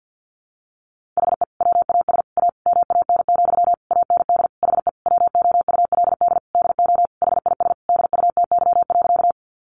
Morse code, heard as a keyed tone, about 50 wpm, 700 Hz. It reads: HE WAS A MAN, AND HE WOULD DO HIS DUTY.
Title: Sentences from Top 1500 Words 50wpm